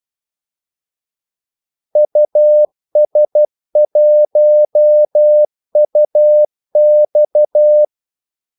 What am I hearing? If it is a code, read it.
Morse code, heard as a keyed tone, about 12 wpm, 600 Hz. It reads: US1UX
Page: 2 ekstra ordmellemrum mellem kaldesignalerne